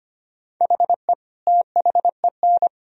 Morse code, heard as a keyed tone, about 25 wpm, 700 Hz. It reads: HE THEN
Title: Sentences from Top 300 Words and Abbreviations 25wpm